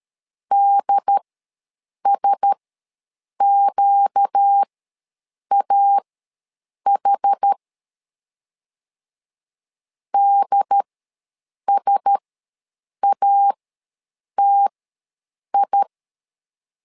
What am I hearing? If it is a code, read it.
Morse code, heard as a keyed tone, about 13 wpm, spaced as for 7 wpm, 800 Hz. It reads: DSQAH DSATI
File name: LEZIONI / 4 Q1S ws 12wpm 1827